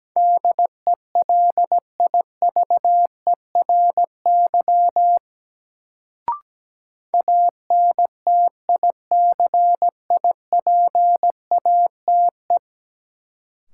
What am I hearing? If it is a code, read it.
Morse code, heard as a keyed tone, about 17 wpm, 700 Hz. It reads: DELIVERY ANTICIPATE
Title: Sentences from Top 1500 Words - New Word Drill Alt - 17wpm